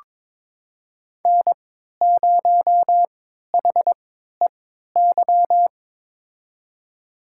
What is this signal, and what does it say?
Morse code, heard as a keyed tone, about 22 wpm, spaced as for 13 wpm, 700 Hz. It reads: N0HEY
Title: USA Call Signs - 5 Characters - 3x Character Spacing - 22wpm